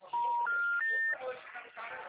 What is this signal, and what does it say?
Special information tone, with speech.
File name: out- -2005- - - .94024